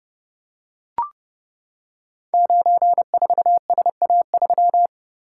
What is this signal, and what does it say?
Morse code, heard as a keyed tone, about 30 wpm, 700 Hz. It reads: 94SA3
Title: Code Groups - 5 Letter:Numbers - 30wpm